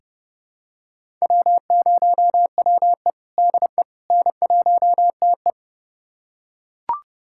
Morse code, heard as a keyed tone, about 32 wpm, 700 Hz. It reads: W0WE DE N1TE
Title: USA Callsign DE Callsign - 4 Characters - 30wpm